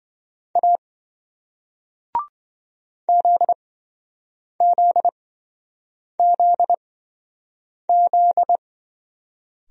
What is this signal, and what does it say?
Morse code, heard as a keyed tone, about 27 wpm, spaced as for 7 wpm, 700 Hz. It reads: A ZZZZ